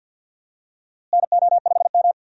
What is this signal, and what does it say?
Morse code, heard as a keyed tone, about 52 wpm, 700 Hz. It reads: NO5M
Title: USA Call Signs - 4 Characters - Repeated 3x - 50wpm